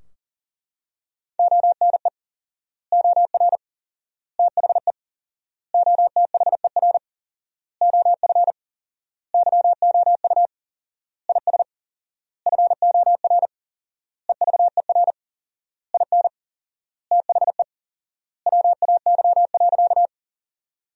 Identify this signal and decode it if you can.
Morse code, heard as a keyed tone, about 40 wpm, 700 Hz. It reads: ONE OR THE OTHER OF YOU IS FOR EVER IN THE WAY.
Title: 4x Word-Spacing - Sentences from Top 400 Words - 40wpm